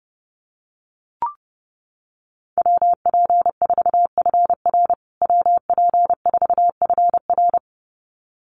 Morse code, heard as a keyed tone, about 30 wpm, 700 Hz. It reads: WP4FR WP4FR
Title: USA Call Signs Repeated 2x 30wpm